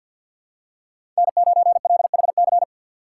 Morse code, heard as a keyed tone, about 53 wpm, 700 Hz. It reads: N9LHC
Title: USA Call Signs - 5 Characters - 4x Character Spacing - 50wpm